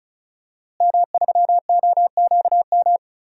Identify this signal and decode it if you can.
Morse code, heard as a keyed tone, about 34 wpm, 700 Hz. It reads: M3OQM
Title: DXCC-100 Call Signs - 5 Characters - Repeated 3x - 35wpm